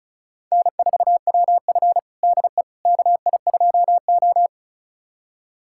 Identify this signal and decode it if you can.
Morse code, heard as a keyed tone, about 36 wpm, 690 Hz. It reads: N4WF DE KI2O
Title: USA Callsign DE Callsign - 4 Characters - 35wpm